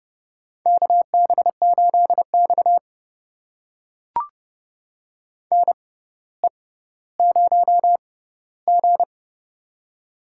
Morse code, heard as a keyed tone, about 30 wpm, spaced as for 15 wpm, 700 Hz. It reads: KB8X NE0G
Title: USA Call Signs - 4 Characters - 6x Character Spacing - 30wpm